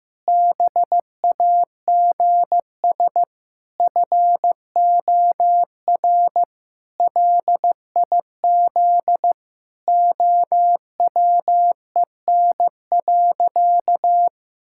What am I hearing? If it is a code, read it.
Morse code, heard as a keyed tone, about 16 wpm, 700 Hz. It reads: BAGS FOR LIZ OWEN.
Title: Pangram Challenge v2 - 15wpm